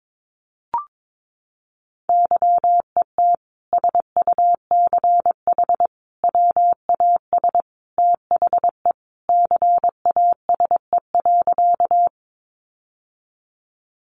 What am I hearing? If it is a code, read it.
Morse code, heard as a keyed tone, about 22 wpm, 700 Hz. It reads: YET SUCH WAS THE CASE.
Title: Sentences from Top 700 Words 22wpm